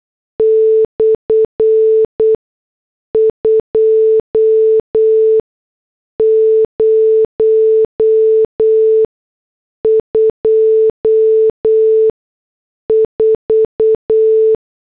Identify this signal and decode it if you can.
Morse code, heard as a keyed tone, about 8 wpm, 440 Hz. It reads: / 2 0 2 4